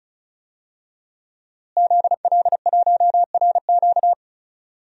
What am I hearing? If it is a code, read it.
Morse code, heard as a keyed tone, about 34 wpm, 700 Hz. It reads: ZL1RQ
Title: DXCC-100 Call Signs - 5 Characters - Repeated 2x - 35wpm